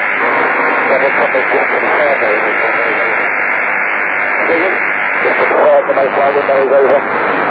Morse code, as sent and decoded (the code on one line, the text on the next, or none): none